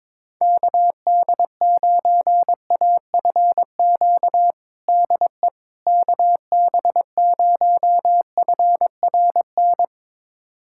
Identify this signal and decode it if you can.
Morse code, heard as a keyed tone, about 22 wpm, 700 Hz. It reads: KD9AFQ DE KB0FRN
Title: USA Callsign DE Callsign - 6 Characters - 22wpm